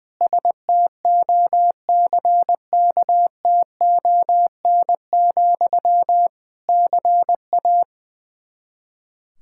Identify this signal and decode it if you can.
Morse code, heard as a keyed tone, about 20 wpm, 700 Hz. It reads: STOCKTON, CA
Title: K1USN SST QSO Practice 20wpm